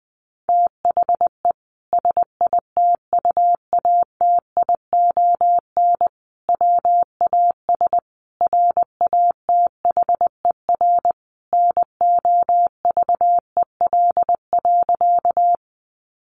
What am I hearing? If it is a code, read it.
Morse code, heard as a keyed tone, about 20 wpm, 700 Hz. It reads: THE SITUATION WAS RATHER NOVEL.
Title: Sentences from Top 1300 Words 20wpm